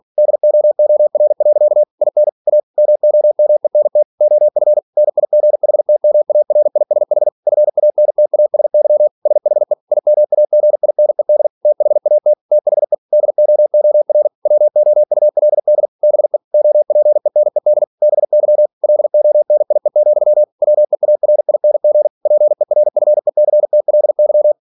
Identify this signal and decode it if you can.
Morse code, heard as a keyed tone, about 46 wpm, 600 Hz. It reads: DOOR. IN A MOMENT OF NIGHTMARISH FANTASY SHE IMAGINED THAT THE DOOR WOULD BE OPENED BY LONIE, WEARING PERFECTLY
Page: Never 47wpm